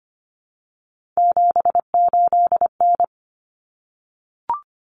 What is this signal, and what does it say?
Morse code, heard as a keyed tone, about 25 wpm, 700 Hz. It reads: 78N
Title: Common Cut Numbers - 3-Digit 25wpm